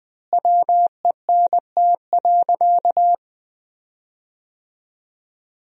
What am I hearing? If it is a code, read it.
Morse code, heard as a keyed tone, about 20 wpm, 700 Hz. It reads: WENT.
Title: 3x Word-Spacing - Sentences from Top 200 Words - 20wpm